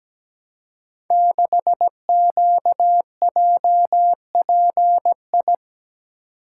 Morse code, heard as a keyed tone, about 17 wpm, 700 Hz. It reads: 6QJPI
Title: Code Groups - 5 Letter:Numbers - 17wpm